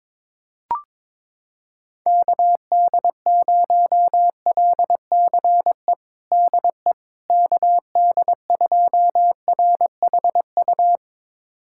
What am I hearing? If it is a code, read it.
Morse code, heard as a keyed tone, about 22 wpm, 700 Hz. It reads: KD0LCE DE KD2RHU
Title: USA Callsign DE Callsign - 6 Characters - 22wpm